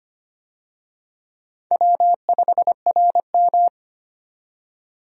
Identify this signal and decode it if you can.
Morse code, heard as a keyed tone, about 25 wpm, 700 Hz. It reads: W5RM
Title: USA Call Signs - 4 Characters - 25wpm